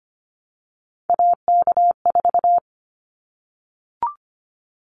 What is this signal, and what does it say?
Morse code, heard as a keyed tone, about 25 wpm, 700 Hz. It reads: AK4
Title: Callsign Prefixes for Top 25 DXCC - 25wpm